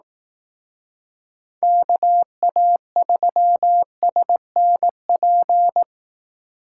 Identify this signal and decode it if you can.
Morse code, heard as a keyed tone, about 18 wpm, 700 Hz. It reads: KA3SNP
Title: Speed Racing - USA 2x3 Call Signs 15wpm